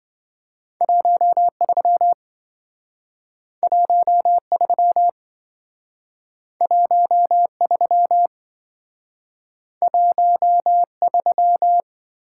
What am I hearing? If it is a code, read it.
Morse code, heard as a keyed tone, about 26 wpm, 700 Hz. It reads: 13 13 13 13